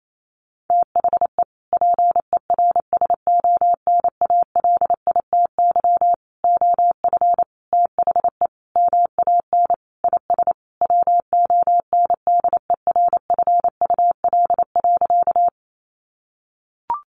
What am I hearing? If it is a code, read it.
Morse code, heard as a keyed tone, about 28 wpm, 700 Hz. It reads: THE PERSONALITY OF THE MAN IS WONDERFUL.
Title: Sentences from Top 1400 Words 28wpm